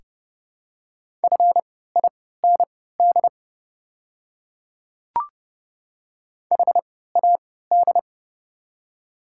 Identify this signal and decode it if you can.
Morse code, heard as a keyed tone, about 30 wpm, spaced as for 17 wpm, 700 Hz. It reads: FIND HAD
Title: 3x Character Spacing - Top 300 Words - 30wpm